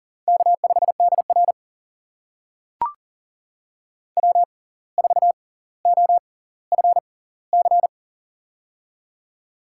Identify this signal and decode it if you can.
Morse code, heard as a keyed tone, about 40 wpm, spaced as for 19 wpm, 700 Hz. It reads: K5DR W4OFC
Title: USA Call Signs - 6x Character Spacing - 40wpm